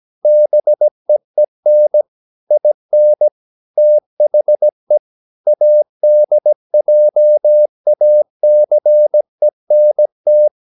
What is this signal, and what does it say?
Morse code, heard as a keyed tone, about 17 wpm, 600 Hz. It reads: BEEN IN THE ADJACENT
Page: Romanen Never af Ken Follet på engelsk.